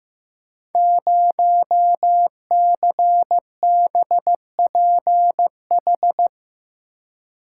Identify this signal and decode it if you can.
Morse code, heard as a keyed tone, about 15 wpm, 700 Hz. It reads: 0CBPH
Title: Code Groups - 5 Letter:Numbers - 15wpm